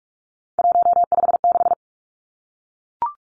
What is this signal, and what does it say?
Morse code, heard as a keyed tone, about 44 wpm, 700 Hz. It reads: 156
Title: Numbers - 3-Digit 45wpm